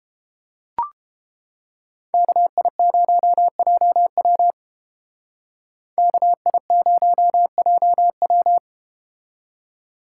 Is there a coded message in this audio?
Yes: Morse code from 2.1 to 8.6 s, heard as a keyed tone, about 33 wpm, 700 Hz. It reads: KI0JW KI0JW